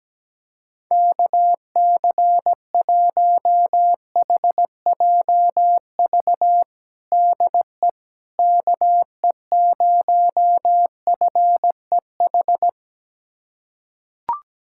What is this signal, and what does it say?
Morse code, heard as a keyed tone, about 17 wpm, 700 Hz. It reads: KC1HJV DE KE0FEH
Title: USA Callsign DE Callsign 17wpm